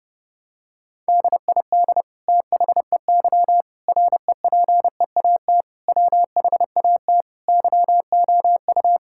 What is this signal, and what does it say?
Morse code, heard as a keyed tone, about 30 wpm, 700 Hz. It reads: DID THEY REPEAT WHAT YOU
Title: Sentences from Top 1600 Words 30wpm